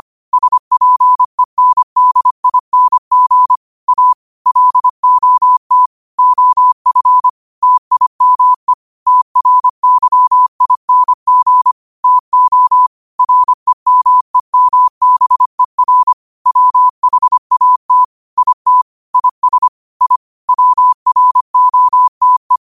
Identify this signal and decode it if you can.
Morse code, heard as a keyed tone, about 25 wpm, 1000 Hz. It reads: SPENDING A LOT OF TIME TRYING TO REMEMBER WHAT IT IS I WROTE